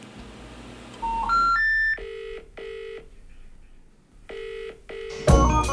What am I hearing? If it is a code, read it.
Special information tone, with noise.